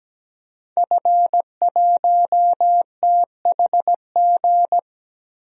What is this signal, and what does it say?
Morse code, heard as a keyed tone, about 17 wpm, 700 Hz. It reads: F1THG
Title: DXCC-100 Call Signs - 5 Characters - Repeated 3x - 17wpm